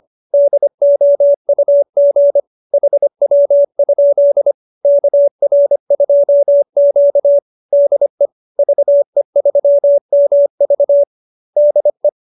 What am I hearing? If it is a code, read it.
Morse code, heard as a keyed tone, about 25 wpm, 590 Hz. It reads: DOUG HW? KR2Q DE VE3MV DE